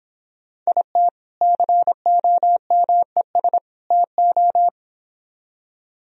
Morse code, heard as a keyed tone, about 31 wpm, 700 Hz. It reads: IT COMES TO